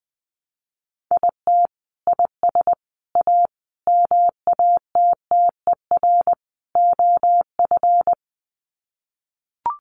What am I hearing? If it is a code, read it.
Morse code, heard as a keyed tone, about 20 wpm, 700 Hz. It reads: IT IS A MATTER OF
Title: Sets of 5 Words Encore2 20wpm